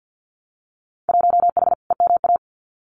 Morse code, heard as a keyed tone, about 50 wpm, 700 Hz. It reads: 1H EMA
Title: Winter Field Day Exchanges 50wpm